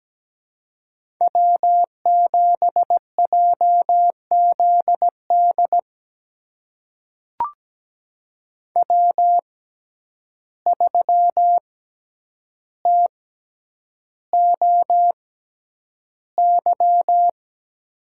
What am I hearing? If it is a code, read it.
Morse code, heard as a keyed tone, about 17 wpm, spaced as for 9 wpm, 700 Hz. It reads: W7JZD W3TOY
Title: USA Call Signs - 5 Characters - 6x Character Spacing - 17wpm